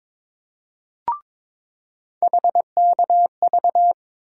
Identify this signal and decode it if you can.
Morse code, heard as a keyed tone, about 22 wpm, 700 Hz. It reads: HKV
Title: USA Call Sign Postfixes - 3 Letters 22wpm